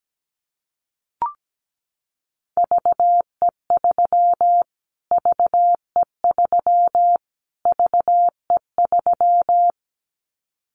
Morse code, heard as a keyed tone, about 17 wpm, 700 Hz. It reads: VE3 VE3 VE3
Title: Callsign Prefixes for Top 25 DXCC - Repeated 3x - 17wpm